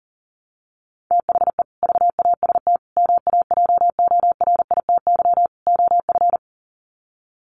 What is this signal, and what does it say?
Morse code, heard as a keyed tone, about 40 wpm, 700 Hz. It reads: THE VAST MAJORITY OF
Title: Sets of 4 Words 40wpm